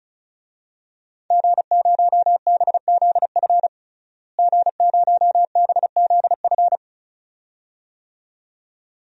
Morse code, heard as a keyed tone, about 36 wpm, 700 Hz. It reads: G0BZF G0BZF
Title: DXCC-100 Call Signs - 5 Characters - Repeated 2x - 35wpm